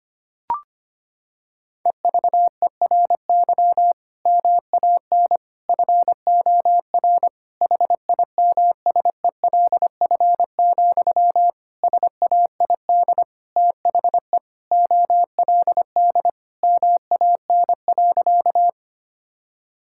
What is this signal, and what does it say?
Morse code, heard as a keyed tone, about 25 wpm, 700 Hz. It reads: EVERY MAN FOR HIMSELF, SAID THE OLD MAN.
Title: Sentences from Top 1600 Words 25wpm